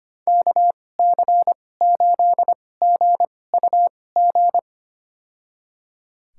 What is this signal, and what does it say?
Morse code, heard as a keyed tone, about 25 wpm, spaced as for 18 wpm, 700 Hz. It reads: KC8GUG
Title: USA Call Signs - 6 Characters - 2x Character Spacing - 25wpm